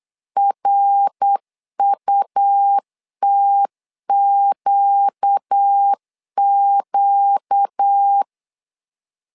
Morse code, heard as a keyed tone, about 8 wpm, 800 Hz. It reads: RUTQQ
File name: 4 Q 8wpm